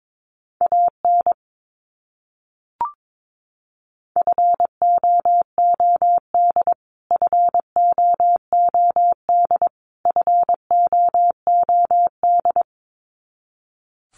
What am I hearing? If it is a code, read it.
Morse code, heard as a keyed tone, about 22 wpm, 700 Hz. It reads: AN FOOD FOOD FOOD
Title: Top 300 Words Repeated 22wpm